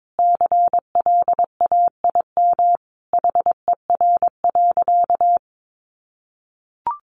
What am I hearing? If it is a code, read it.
Morse code, heard as a keyed tone, about 22 wpm, 700 Hz. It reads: CLAIM HER.
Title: Sentences from Top 1500 Words 22wpm